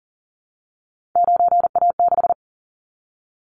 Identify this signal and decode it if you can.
Morse code, heard as a keyed tone, about 37 wpm, 700 Hz. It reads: 9A6
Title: Callsign Prefixes for Top 50 DXCC - 40wpm